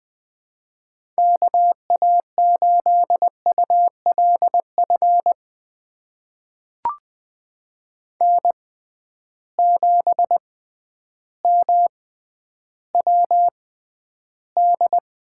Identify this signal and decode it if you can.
Morse code, heard as a keyed tone, about 20 wpm, spaced as for 11 wpm, 700 Hz. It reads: KA8ULF N7MWD
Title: USA Call Signs - 6x Character Spacing - 20wpm